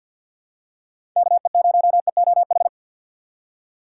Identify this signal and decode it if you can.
Morse code, heard as a keyed tone, about 48 wpm, 700 Hz. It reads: KE0EOH
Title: USA Call Signs - 4x Character Spacing - 50wpm